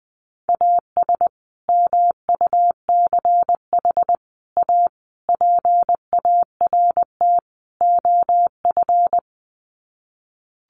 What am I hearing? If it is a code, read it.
Morse code, heard as a keyed tone, about 20 wpm, 700 Hz. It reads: AS MUCH A PART OF
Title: Sets of 5 Words Encore2 20wpm